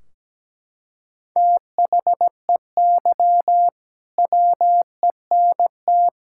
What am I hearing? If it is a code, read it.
Morse code, heard as a keyed tone, about 17 wpm, 700 Hz. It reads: THEY WENT
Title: Sentences from Top 1600 Words 17wpm